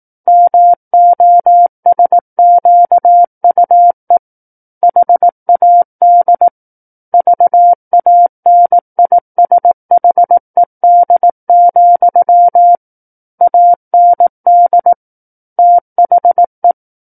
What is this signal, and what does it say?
Morse code, heard as a keyed tone, about 18 wpm, 700 Hz. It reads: MOSQUE HAD VANISHED, AND THE
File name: Book One / 09-Chapter 9 - 18 WPM